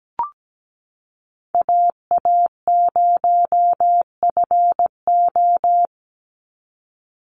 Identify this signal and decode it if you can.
Morse code, heard as a keyed tone, about 17 wpm, 700 Hz. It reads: AA0FO
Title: Call Signs Repeated 3x - 17wpm